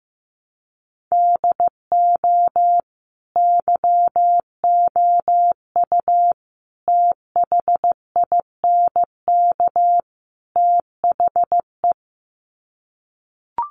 Morse code, heard as a keyed tone, about 15 wpm, 700 Hz. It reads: DO YOU THINK THE
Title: Sets of 4 Words 15wpm